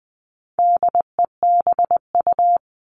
Morse code, heard as a keyed tone, about 20 wpm, 700 Hz. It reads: DEBU
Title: All Cut Numbers - 4-Digit 20wpm